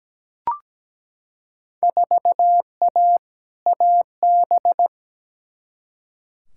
Morse code, heard as a keyed tone, about 17 wpm, 700 Hz. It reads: 4A AB
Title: Field Day Exchanges Encore - 17wpm